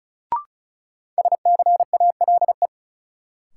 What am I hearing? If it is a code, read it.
Morse code, heard as a keyed tone, about 34 wpm, 700 Hz. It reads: SCALE
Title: Sentences from Top 1000 Words - Review Part 4 - New Word Drill - 35wpm